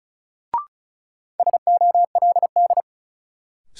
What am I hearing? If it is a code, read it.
Morse code, heard as a keyed tone, about 36 wpm, 700 Hz. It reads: SOLD
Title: Sentences from Top 1400 Words - New Word Drill Alt - 35wpm